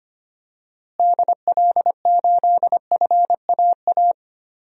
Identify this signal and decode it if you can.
Morse code, heard as a keyed tone, about 25 wpm, 700 Hz. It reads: DL8FAA
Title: DXCC-100 Call Signs - 6 Characters - Repeated 2x - 25wpm